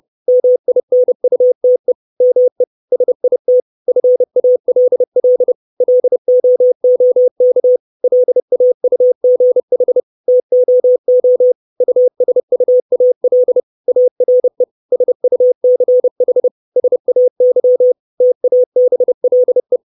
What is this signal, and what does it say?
Morse code, heard as a keyed tone, about 30 wpm, 500 Hz. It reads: MINUTE ME SIT FALL LOOK LAUGH TOO USUAL ARE SUCH SAY TABLE
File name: 632 words / 632 Words 30WPM